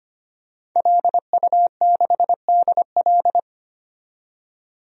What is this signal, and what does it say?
Morse code, heard as a keyed tone, about 25 wpm, 700 Hz. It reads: LU6DL
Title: DXCC-100 Call Signs - 5 Characters - 25wpm